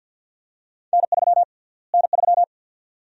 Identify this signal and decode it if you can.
Morse code, heard as a keyed tone, about 51 wpm, 700 Hz. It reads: N3 N3
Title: USA Call Sign Prefixes - Repeated 2x - 50wpm